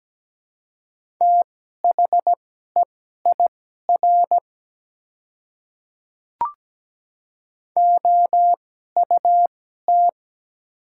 Morse code, heard as a keyed tone, about 17 wpm, spaced as for 12 wpm, 700 Hz. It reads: THEIR OUT
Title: 2x Character Spacing - Top 200 Words - 17wpm